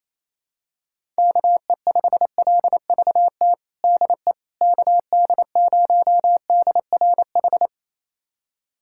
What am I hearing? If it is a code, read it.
Morse code, heard as a keyed tone, about 28 wpm, 700 Hz. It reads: KE5LVT DE KD0DRH
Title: USA Callsign DE Callsign - 6 Characters - 28wpm